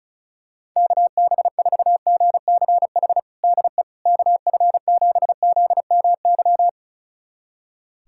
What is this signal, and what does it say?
Morse code, heard as a keyed tone, about 34 wpm, 690 Hz. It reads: KB4GCH DE KF7ZMY
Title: USA Callsign DE Callsign 35wpm